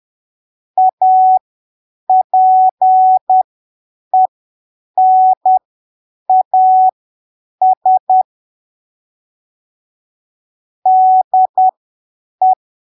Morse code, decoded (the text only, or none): APENAS DE